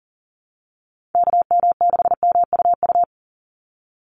Morse code, heard as a keyed tone, about 40 wpm, 700 Hz. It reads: KM6MUU
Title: USA Call Signs 40wpm